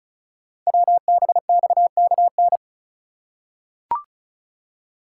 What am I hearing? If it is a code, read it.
Morse code, heard as a keyed tone, about 36 wpm, 700 Hz. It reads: WBXKN
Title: Code Groups - 5 Letters - 35wpm